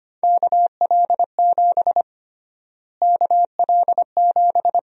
Morse code, heard as a keyed tone, about 25 wpm, 700 Hz. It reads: KL7 KL7
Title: USA Call Sign Prefixes - Repeated 2x - 25wpm